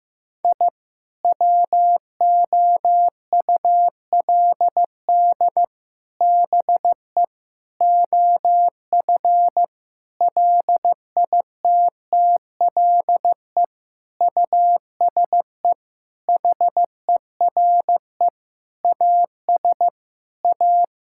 Morse code, heard as a keyed tone, about 16 wpm, 700 Hz. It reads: I WOULD BE OF LITTLE USE HERE AS A
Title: Sentences from Top 1000 Words - Review Part 5 - 15wpm